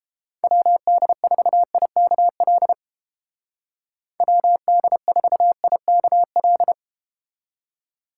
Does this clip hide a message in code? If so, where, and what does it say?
Yes: Morse code from 0.4 to 6.7 s, heard as a keyed tone, about 32 wpm, 700 Hz. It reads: WD4IKL WD4IKL